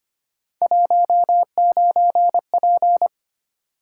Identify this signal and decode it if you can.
Morse code, heard as a keyed tone, about 29 wpm, 700 Hz. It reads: 19P
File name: Speed-Racing - Three Letter-Number Combinations - 25wpm